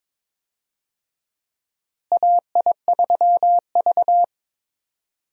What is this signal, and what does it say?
Morse code, heard as a keyed tone, about 21 wpm, 700 Hz. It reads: AI3V
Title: USA Call Signs - 4 Characters - Repeated 2x - 22wpm